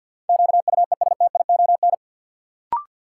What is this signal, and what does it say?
Morse code, heard as a keyed tone, about 53 wpm, 700 Hz. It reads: QUESTION
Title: Sentences from Top 400 Words - New Word Drill - 50wpm